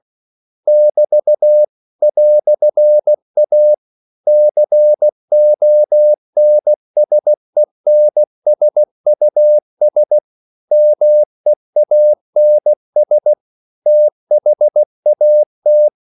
Morse code, heard as a keyed tone, about 16 wpm, 600 Hz. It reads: <BT> "A CONSENSUS MEANS THAT
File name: CW / Practice / EnglishQuotations0416